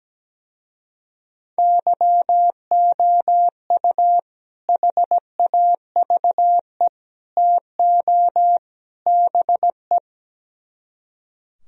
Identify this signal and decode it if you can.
Morse code, heard as a keyed tone, about 17 wpm, 700 Hz. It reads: YOU HAVE TO BE
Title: Speed-Racing - Sets of 4 Words 17wpm